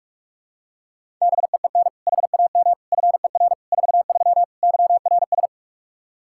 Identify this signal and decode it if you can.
Morse code, heard as a keyed tone, about 46 wpm, 700 Hz. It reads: BEEN HAM FER 43 YRS
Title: Practice QSO - Exchange 3 - More Info - 45wpm